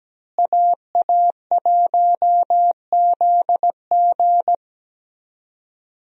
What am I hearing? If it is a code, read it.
Morse code, heard as a keyed tone, about 17 wpm, 700 Hz. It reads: AA1ZG
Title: USA Call Signs - 5 Characters - Repeated 3x - 17wpm